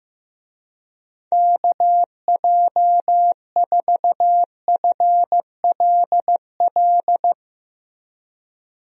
Morse code, heard as a keyed tone, about 15 wpm, 700 Hz. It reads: KJ4FLL
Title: USA Call Signs - 6 Characters - Repeated 3x - 15wpm